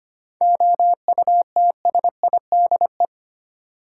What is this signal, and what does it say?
Morse code, heard as a keyed tone, about 25 wpm, 700 Hz. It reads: OUTSIDE
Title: Sentences from Top 900 Words - New Word Drill - 25wpm